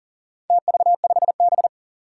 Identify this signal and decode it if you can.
Morse code, heard as a keyed tone, about 37 wpm, 700 Hz. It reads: TV5B
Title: Four Challenging Letter-Number Combinations 40wpm